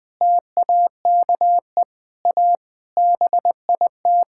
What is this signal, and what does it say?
Morse code, heard as a keyed tone, about 20 wpm, 700 Hz. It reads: TAKE A BIT
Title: Sentences from Top 1000 Words - Review Part 5 - 20wpm